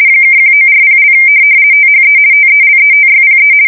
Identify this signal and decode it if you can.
RTTY, about 50 baud. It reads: (1940) THE TELETYPE
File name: fsk170-50B / teletype-corp